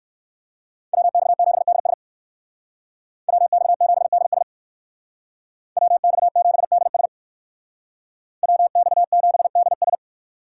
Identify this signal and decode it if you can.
Morse code, heard as a keyed tone, about 59 wpm, 700 Hz. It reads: WX7DS WX7DS WX7DS WX7DS